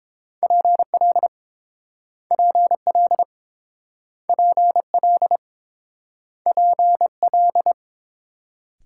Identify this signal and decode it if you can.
Morse code, heard as a keyed tone, about 30 wpm, 700 Hz. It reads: PL PL PL PL